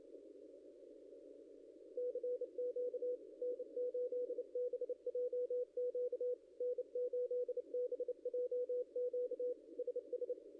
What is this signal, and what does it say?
Morse code, heard as a keyed tone, about 28 wpm, 500 Hz. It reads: CQ N8BJQ N8BJQ SS
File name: contests / ss-cw-k5tr-op-n5rz -0600